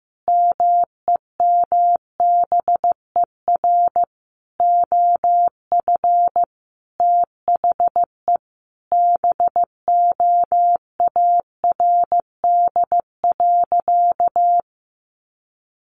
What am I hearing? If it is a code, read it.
Morse code, heard as a keyed tone, about 15 wpm, 700 Hz. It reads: MEMBER OF THE BOARD.
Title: Sentences from Top 1100 Words 15wpm